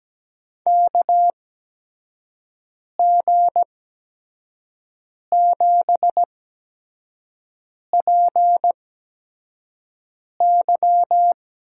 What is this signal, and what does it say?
Morse code, heard as a keyed tone, about 17 wpm, spaced as for 5 wpm, 700 Hz. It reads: KG7PY
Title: USA Call Signs - 6 Characters - 8x Character Spacing - 17wpm